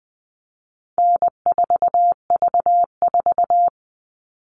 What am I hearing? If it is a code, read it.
Morse code, heard as a keyed tone, about 20 wpm, 700 Hz. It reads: N4V4
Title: All Cut Numbers - 4-Digit 20wpm